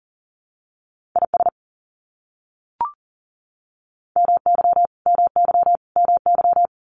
Morse code, heard as a keyed tone, about 40 wpm, 700 Hz. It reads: IS MY MY MY
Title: Top 500 Words Repeated 40wpm